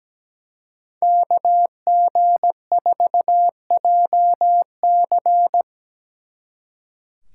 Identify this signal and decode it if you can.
Morse code, heard as a keyed tone, about 20 wpm, 700 Hz. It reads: KG4JC
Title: Speed Racing - USA 2x2 Call Signs 17wpm